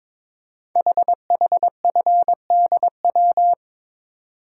Morse code, heard as a keyed tone, about 22 wpm, 700 Hz. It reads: HHFDW
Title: Code Groups - 5 Letters - 22wpm